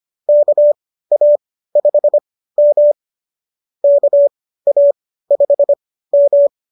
Morse code, heard as a keyed tone, about 25 wpm, spaced as for 15 wpm, 570 Hz. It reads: KA5M KA5M